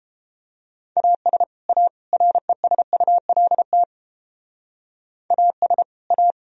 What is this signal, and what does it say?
Morse code, heard as a keyed tone, about 29 wpm, 700 Hz. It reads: AS A RESULT AS A